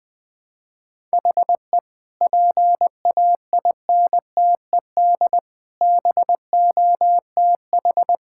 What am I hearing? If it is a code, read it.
Morse code, heard as a keyed tone, about 21 wpm, 700 Hz. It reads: HE PAINTED BOTH
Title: Sentences with 4-Word Phrases - 20wpm